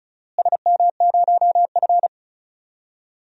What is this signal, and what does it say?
Morse code, heard as a keyed tone, about 36 wpm, 700 Hz. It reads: SM0F
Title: DXCC-100 Call Signs - 4 Characters - Repeated 2x - 35wpm